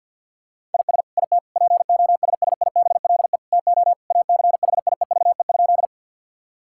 Morse code, heard as a keyed tone, about 50 wpm, 700 Hz. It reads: IS IT POSSIBLE TO ACHIEVE?
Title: Sentences from Top 1000 Words - Review Part 6 - 50wpm